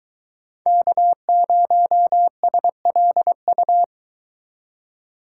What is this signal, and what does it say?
Morse code, heard as a keyed tone, about 23 wpm, 700 Hz. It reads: K0SLU